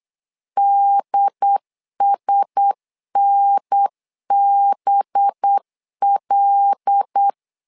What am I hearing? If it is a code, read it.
Morse code, heard as a keyed tone, about 8 wpm, 800 Hz. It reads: DSNBL
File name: LEZIONI / 5 LBVC / 5 LBVC 8wpm 2935